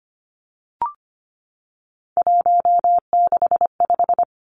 Morse code, heard as a keyed tone, about 25 wpm, 700 Hz. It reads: 165
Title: Three Challenging Letter-Number Combinations 25wpm